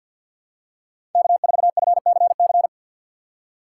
Morse code, heard as a keyed tone, about 51 wpm, 700 Hz. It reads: K4FCC
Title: K1USN SST QSO Practice - 50wpm